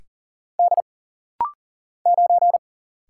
Morse code, heard as a keyed tone, about 40 wpm, 700 Hz. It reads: D 9
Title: Single Letter-Number Alt - Rapid-Fire 40wpm